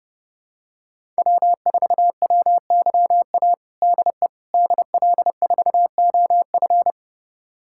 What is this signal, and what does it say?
Morse code, heard as a keyed tone, about 30 wpm, 700 Hz. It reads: W4WYA DE DL4OF
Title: USA Callsign DE Callsign - 5 Characters - 30wpm